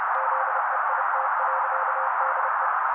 Morse code, heard as a keyed tone, about 30 wpm, 530 Hz. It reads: ZEIT.DE
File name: daily / 29wpm noise